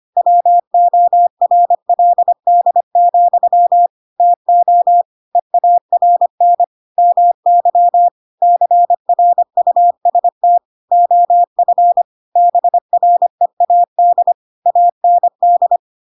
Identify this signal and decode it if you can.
Morse code, heard as a keyed tone, about 25 wpm, 700 Hz. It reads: WORLD, TO EARN MY CRUST OF BREAD AND
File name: Pinocchio / Pinocchio ch2 25 25 ik1hge